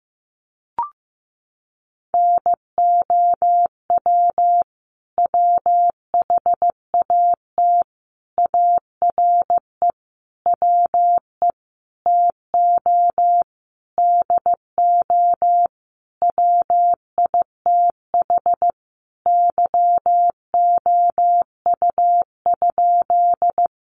Morse code, heard as a keyed tone, about 15 wpm, 700 Hz. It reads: NOW WHAT ARE WE TO DO WITH YOU?
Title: Sentences from Top 100 Words 15wpm